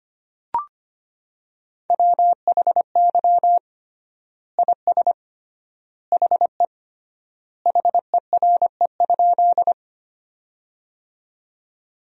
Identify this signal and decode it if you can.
Morse code, heard as a keyed tone, about 25 wpm, 700 Hz. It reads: WHY IS HE HERE?
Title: 3x Word-Spacing - Sentences from Top 200 Words - 25wpm